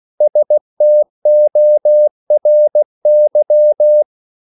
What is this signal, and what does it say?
Morse code, heard as a keyed tone, about 16 wpm, 600 Hz. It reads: STORY
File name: CW / Practice / 500MostCommonEnglishWords16